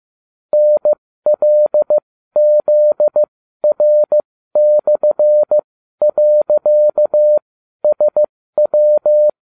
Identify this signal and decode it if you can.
Morse code, heard as a keyed tone, about 15 wpm, spaced as for 12 wpm, 600 Hz. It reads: NLZR/.SW